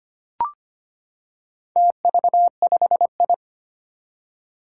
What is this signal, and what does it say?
Morse code, heard as a keyed tone, about 25 wpm, 700 Hz. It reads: TV5I
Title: Four Challenging Letter-Number Combinations 25wpm